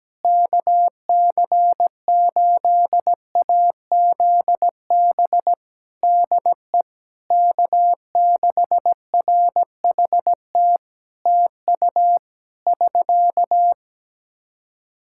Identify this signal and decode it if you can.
Morse code, heard as a keyed tone, about 17 wpm, 700 Hz. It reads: KC8AZB DE K6RHT TU <SK>
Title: Practice QSO - Exchange 4 - Ending - 17wpm